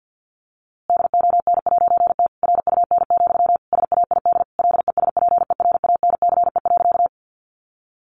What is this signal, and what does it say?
Morse code, heard as a keyed tone, about 50 wpm, 700 Hz. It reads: DON'T RUN, SAID LESPERANCE.
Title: A Sound Of Thunder 50wpm